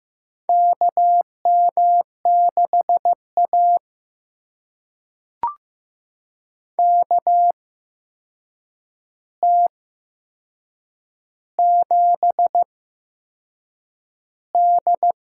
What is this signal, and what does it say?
Morse code, heard as a keyed tone, about 15 wpm, spaced as for 6 wpm, 700 Hz. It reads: KM6A KT7D
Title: USA Call Signs - 4 Characters - 8x Character Spacing - 15wpm